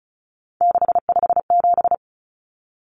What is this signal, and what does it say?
Morse code, heard as a keyed tone, about 34 wpm, 700 Hz. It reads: 657
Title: Common Cut Numbers - 3-Digit 35wpm